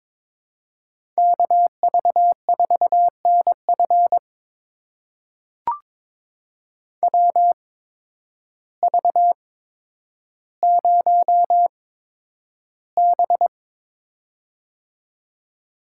Morse code, heard as a keyed tone, about 22 wpm, spaced as for 10 wpm, 700 Hz. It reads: KV4NF WV0B
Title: USA Call Signs - 8x Character Spacing - 22wpm